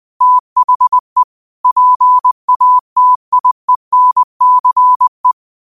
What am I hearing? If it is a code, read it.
Morse code, heard as a keyed tone, about 20 wpm, 1000 Hz. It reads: THE PATIENCE
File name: qotdfiles / 20wpm